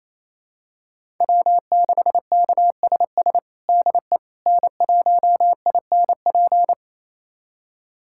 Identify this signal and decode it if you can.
Morse code, heard as a keyed tone, about 27 wpm, 690 Hz. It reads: W6KSS DE N1INP
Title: USA Callsign DE Callsign - 5 Characters - 28wpm